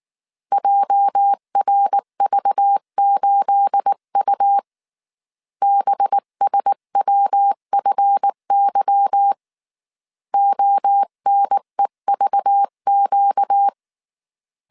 Morse code, heard as a keyed tone, about 19 wpm, 800 Hz. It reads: JRV8U BSWFY ONEVQ
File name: CW / DET / 1 9 1500C S B 18wpm 2551